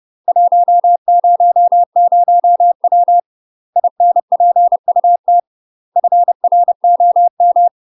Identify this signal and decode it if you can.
Morse code, heard as a keyed tone, about 30 wpm, 700 Hz. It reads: 100W INPUT FROM
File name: UK MASTER QSOs TRAINING / 30WPM / UK MASTER QSOs 0021